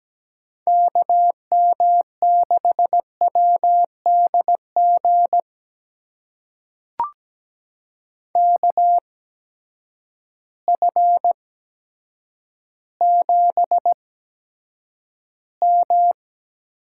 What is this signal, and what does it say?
Morse code, heard as a keyed tone, about 17 wpm, spaced as for 8 wpm, 700 Hz. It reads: KM6WDG KF7M
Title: USA Call Signs - 6 Characters - 8x Character Spacing - 17wpm